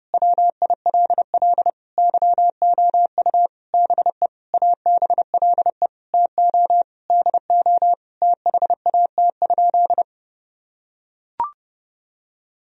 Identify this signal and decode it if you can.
Morse code, heard as a keyed tone, about 30 wpm, 700 Hz. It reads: WILL YOU BE ABLE TO DO THAT?
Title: Sentences from Top 1000 Words - Review Part 2 - 30wpm